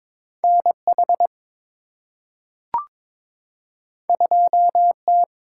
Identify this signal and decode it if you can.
Morse code, heard as a keyed tone, about 22 wpm, 700 Hz. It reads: NH 2T
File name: practice / 22wpm / Two Letter-Number Combinations - 22wpm